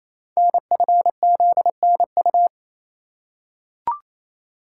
Morse code, heard as a keyed tone, about 29 wpm, 700 Hz. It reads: NFZNU
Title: Code Groups - 5 Letters - 28wpm